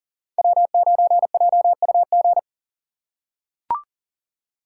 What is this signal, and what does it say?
Morse code, heard as a keyed tone, about 40 wpm, 700 Hz. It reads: W9JUG
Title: K1USN SST QSO Practice - 40wpm